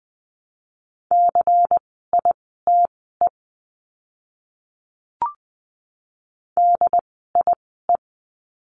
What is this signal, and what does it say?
Morse code, heard as a keyed tone, about 20 wpm, spaced as for 14 wpm, 700 Hz. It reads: CITE DIE
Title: 094 Lesson 16f - Letter c - Words 20wpm